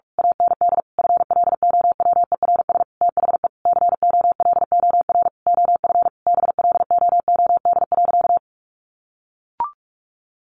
Morse code, heard as a keyed tone, about 44 wpm, 700 Hz. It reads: AND FLOWERS THE COLOR OF BLOOD.
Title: A Sound Of Thunder 45wpm